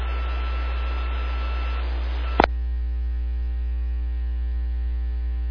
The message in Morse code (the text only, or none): none